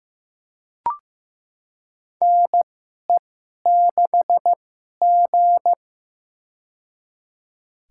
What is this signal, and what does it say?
Morse code, heard as a keyed tone, about 15 wpm, spaced as for 11 wpm, 700 Hz. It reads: NE6G
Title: USA Call Signs - 4 Characters - 2x Character Spacing - 15wpm